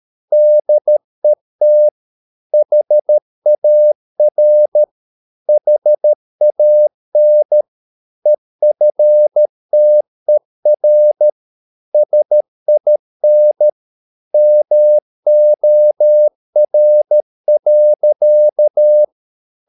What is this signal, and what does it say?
Morse code, heard as a keyed tone, about 13 wpm, 600 Hz. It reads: DET HAR HAN EFTER SIN MOR.
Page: Ferie 13 wpm